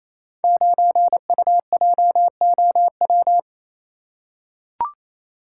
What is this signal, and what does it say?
Morse code, heard as a keyed tone, about 28 wpm, 700 Hz. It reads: 9UJOW
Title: Code Groups - 5 Letter:Numbers - 28wpm